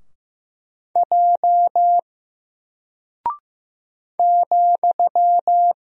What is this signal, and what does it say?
Morse code, heard as a keyed tone, about 15 wpm, 700 Hz. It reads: J ,
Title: Single Character (Letter:Number:Punctuation:Prosign) 15wpm